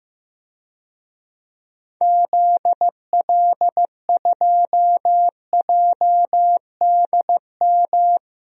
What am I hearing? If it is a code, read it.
Morse code, heard as a keyed tone, about 15 wpm, 700 Hz. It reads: ZL2JDM
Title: DXCC-100 Call Signs - 6 Characters - 15wpm